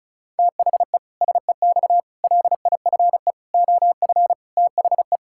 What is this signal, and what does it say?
Morse code, heard as a keyed tone, about 34 wpm, 700 Hz. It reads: THE SEX LIFE OF THE
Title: Pangram Challenge v2 - 35wpm